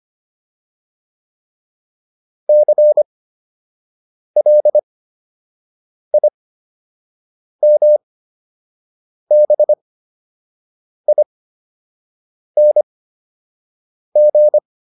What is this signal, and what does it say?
Morse code, heard as a keyed tone, about 25 wpm, spaced as for 6 wpm, 600 Hz. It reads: CLIMBING